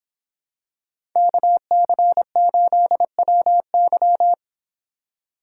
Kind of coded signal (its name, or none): Morse code